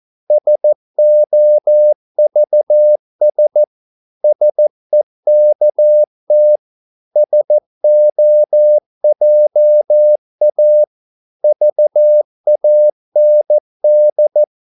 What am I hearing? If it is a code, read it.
Morse code, heard as a keyed tone, about 14 wpm, 600 Hz. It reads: SOVS SEKT SOJA VAND
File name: Korte ord DK 14wpm